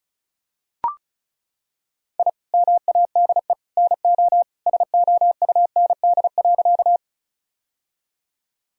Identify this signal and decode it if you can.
Morse code, heard as a keyed tone, about 34 wpm, 700 Hz. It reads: I MADE NO SOUND.
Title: Sentences from Top 1000 Words - Review Part 1 - 35wpm